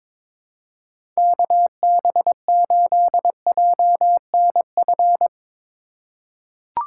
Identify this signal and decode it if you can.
Morse code, heard as a keyed tone, about 22 wpm, 700 Hz. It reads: KB8JNF
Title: USA Call Signs - 6 Characters - 4x Character Spacing - 22wpm